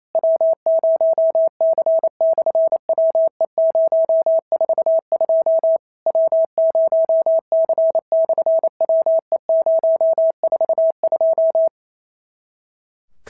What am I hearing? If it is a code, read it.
Morse code, heard as a keyed tone, about 28 wpm, 650 Hz. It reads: W0C/WE042 W0C/WE042
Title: SOTA QSO Practice 28wpm